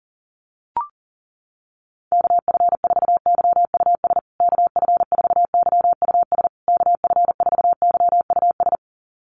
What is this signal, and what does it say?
Morse code, heard as a keyed tone, about 37 wpm, 700 Hz. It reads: KF4YUS KF4YUS KF4YUS
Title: USA Call Signs Repeated 3x 40wpm